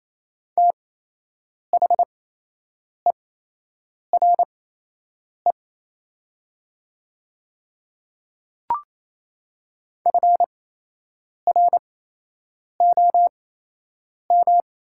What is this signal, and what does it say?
Morse code, heard as a keyed tone, about 28 wpm, spaced as for 8 wpm, 690 Hz. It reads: THERE FROM
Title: 8x Character Spacing - Top 100 Words - 28wpm